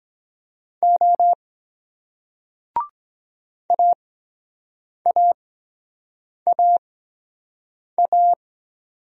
Morse code, heard as a keyed tone, about 23 wpm, spaced as for 7 wpm, 700 Hz. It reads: O AAAA